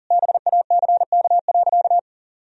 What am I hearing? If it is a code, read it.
Morse code, heard as a keyed tone, about 40 wpm, 700 Hz. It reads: BACK.
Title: Sentences from Top 1000 Words - Review Part 2 - 40wpm